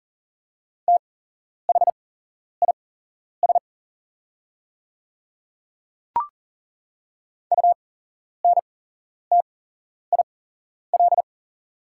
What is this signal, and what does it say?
Morse code, heard as a keyed tone, about 40 wpm, spaced as for 11 wpm, 700 Hz. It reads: THIS UNTIL
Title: 8x Character Spacing - Top 500 Words - 40wpm